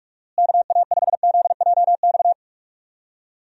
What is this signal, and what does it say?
Morse code, heard as a keyed tone, about 41 wpm, 700 Hz. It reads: KA5ZJX
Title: USA Call Signs - 6 Characters - Repeated 3x - 45wpm